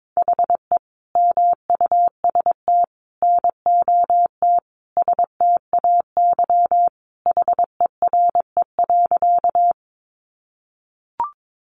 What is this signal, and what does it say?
Morse code, heard as a keyed tone, about 22 wpm, 700 Hz. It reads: HE MUST NOT STAY HERE.
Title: Sentences from Top 600 Words 22wpm